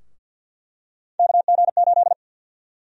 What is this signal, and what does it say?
Morse code, heard as a keyed tone, about 50 wpm, 700 Hz. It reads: KG8
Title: USA Call Sign Prefixes - Repeated 2x - 50wpm